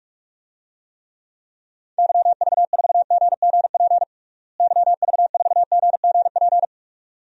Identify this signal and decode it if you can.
Morse code, heard as a keyed tone, about 46 wpm, 700 Hz. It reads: YV4GGP YV4GGP
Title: DXCC-100 Call Signs - 6 Characters - Repeated 2x - 45wpm